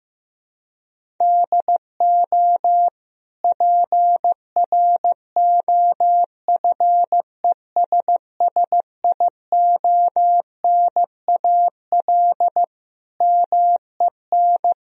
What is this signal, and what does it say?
Morse code, heard as a keyed tone, about 15 wpm, 700 Hz. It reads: DO PROFESSIONAL MEN
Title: Sentences from Top 1000 Words - Review Part 4 - 15wpm